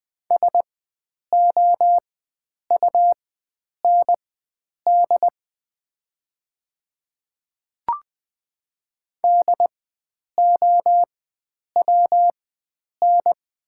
Morse code, heard as a keyed tone, about 20 wpm, spaced as for 9 wpm, 700 Hz. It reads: SOUND DOWN
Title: 4x Character Spacing - Top 100 Words - 20wpm